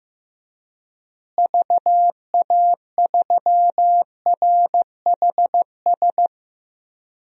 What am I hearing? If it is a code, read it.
Morse code, heard as a keyed tone, about 15 wpm, 700 Hz. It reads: VA3RHS
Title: DXCC-100 Call Signs - 6 Characters - Repeated 2x - 15wpm